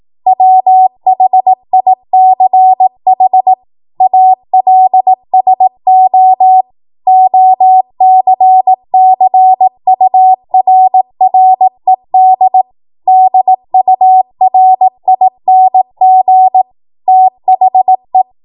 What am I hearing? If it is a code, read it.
Morse code, heard as a keyed tone, about 18 wpm, 750 Hz. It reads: WHICH ALSO OCCURRED DURING THE
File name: Morse / 18WPM